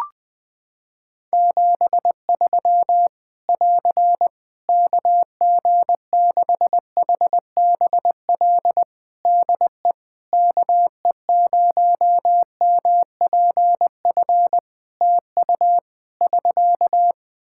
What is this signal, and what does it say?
Morse code, heard as a keyed tone, about 20 wpm, 700 Hz. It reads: 73 <AR> KG6HBL DE KE0MPF TU <SK>
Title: Practice QSO - Exchange 4 - Ending - 20wpm